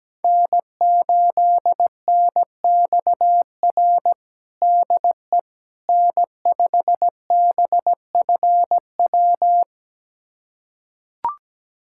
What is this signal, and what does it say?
Morse code, heard as a keyed tone, about 17 wpm, 700 Hz. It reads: N8NXR DE N5BFW
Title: USA Callsign DE Callsign - 5 Characters - 17wpm